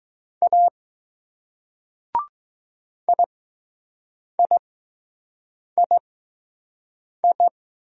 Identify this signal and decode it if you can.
Morse code, heard as a keyed tone, about 20 wpm, spaced as for 6 wpm, 700 Hz. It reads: A IIII